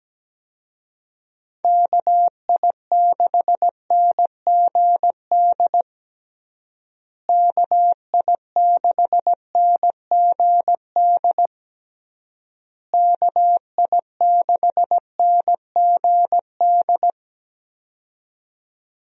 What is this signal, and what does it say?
Morse code, heard as a keyed tone, about 17 wpm, 700 Hz. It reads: KI6NGD KI6NGD KI6NGD
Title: USA Call Signs - 6 Characters - Repeated 3x - 17wpm